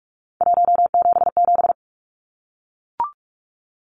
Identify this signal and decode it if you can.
Morse code, heard as a keyed tone, about 44 wpm, 700 Hz. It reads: 177
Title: Numbers - 3-Digit 45wpm